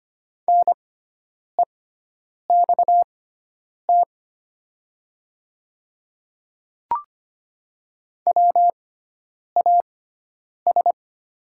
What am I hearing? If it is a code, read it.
Morse code, heard as a keyed tone, about 25 wpm, spaced as for 9 wpm, 690 Hz. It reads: NEXT WAS
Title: 6x Character Spacing - Top 300 Words - 25wpm